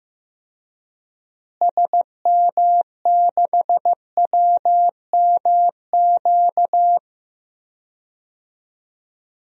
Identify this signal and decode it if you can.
Morse code, heard as a keyed tone, about 15 wpm, 700 Hz. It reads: SM6WMQ
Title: DXCC-100 Call Signs - 6 Characters - Repeated 3x - 15wpm